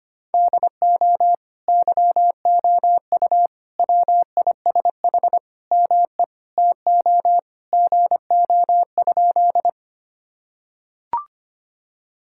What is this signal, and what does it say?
Morse code, heard as a keyed tone, about 25 wpm, 700 Hz. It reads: DO YOU WISH ME TO GO?
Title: Sentences from Top 1600 Words 25wpm